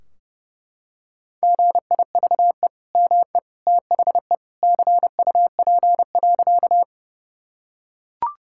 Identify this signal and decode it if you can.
Morse code, heard as a keyed tone, about 30 wpm, 700 Hz. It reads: GIVE ME THE CUP.
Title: Sentences from Top 1600 Words 30wpm